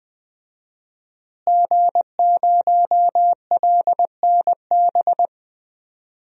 Morse code, heard as a keyed tone, about 21 wpm, 700 Hz. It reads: G0LNB
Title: DXCC-100 Call Signs - 5 Characters - Repeated 3x - 20wpm